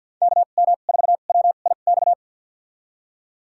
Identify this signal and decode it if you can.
Morse code, heard as a keyed tone, about 51 wpm, spaced as for 36 wpm, 700 Hz. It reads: KM4WIX
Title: USA Call Signs - 2x Character Spacing - 50wpm